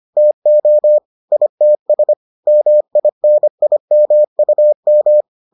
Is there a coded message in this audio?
Yes: Morse code from 0.2 to 5.2 s, heard as a keyed tone, about 25 wpm, 600 Hz. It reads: TO ITS MINIMUM